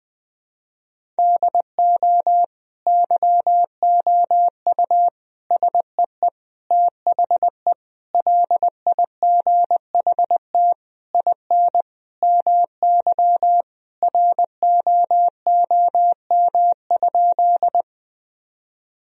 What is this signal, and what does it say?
Morse code, heard as a keyed tone, about 20 wpm, 700 Hz. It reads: DO YOU SEE THE LIGHT IN MY ROOM?
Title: Sentences from Top 1000 Words - Review Part 1 - 20wpm